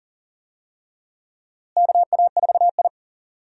Morse code, heard as a keyed tone, about 40 wpm, 700 Hz. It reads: KA4I
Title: USA Call Signs - 4 Characters - Repeated 3x - 40wpm